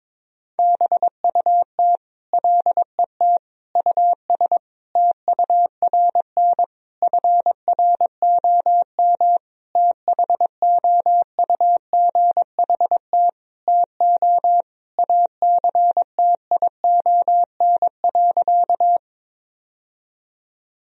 Morse code, heard as a keyed tone, about 22 wpm, 700 Hz. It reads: BUT LET US TURN FROM THOUGHT TO ACTION.
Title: Sentences from Top 1000 Words - Review Part 4 - 22wpm